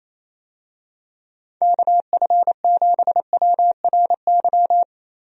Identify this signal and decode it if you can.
Morse code, heard as a keyed tone, about 28 wpm, 700 Hz. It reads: KF7WRY
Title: USA Call Signs - 6 Characters - 28wpm